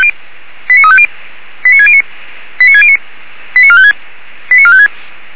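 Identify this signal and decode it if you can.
Five-tone selective call (ZVEI), heard as quick runs of five tones, tones of about 70 ms. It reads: 89350 89609 89609 89456 89456
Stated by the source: ZVEI five-tone selective call, received on an SDR.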